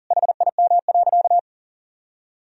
Morse code, heard as a keyed tone, about 37 wpm, 700 Hz. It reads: HIM.
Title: Sentences from Top 1000 Words - Review Part 2 - 40wpm